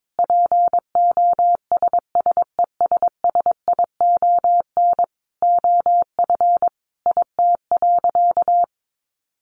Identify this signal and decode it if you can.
Morse code, heard as a keyed tone, about 22 wpm, 700 Hz. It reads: POSSESSION OF IT.
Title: Sentences from Top 1400 Words 22wpm